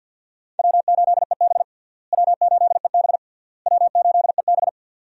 Morse code, heard as a keyed tone, about 50 wpm, 700 Hz. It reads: W8EB W8EB W8EB
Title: USA Call Signs - 4 Characters - Repeated 3x - 50wpm